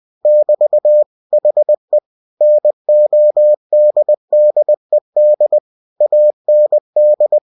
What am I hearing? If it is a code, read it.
Morse code, heard as a keyed tone, about 20 wpm, 600 Hz. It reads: <BT> HE NODDED AND
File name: cw-books / winnie-the-pooh / winnie-20wpm-page-0111